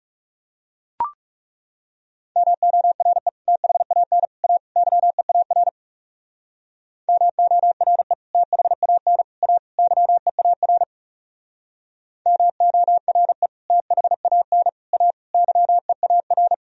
Morse code, heard as a keyed tone, about 37 wpm, 700 Hz. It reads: MORE THAN A YEAR MORE THAN A YEAR MORE THAN A YEAR
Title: Speed-Racing - Sets of 4 Words 30wpm